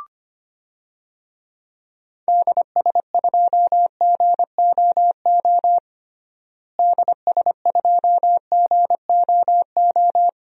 Morse code, heard as a keyed tone, about 25 wpm, 700 Hz. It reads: DS2GOO DS2GOO
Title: DXCC-100 Call Signs - 6 Characters - Repeated 3x - 25wpm